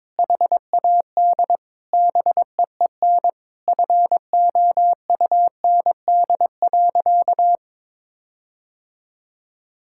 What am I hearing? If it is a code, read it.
Morse code, heard as a keyed tone, about 22 wpm, 700 Hz. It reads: HAD BEEN FOUND.
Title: Sentences from Top 1600 Words 22wpm